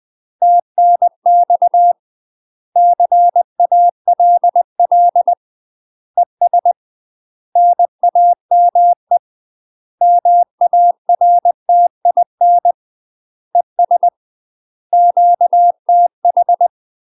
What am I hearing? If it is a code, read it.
Morse code, heard as a keyed tone, about 20 wpm, 700 Hz. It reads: TNX CALL ES NAME MARTIN ES QTH
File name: UK MASTER QSOs TRAINING / 20WPM / UK MASTER QSOs 0039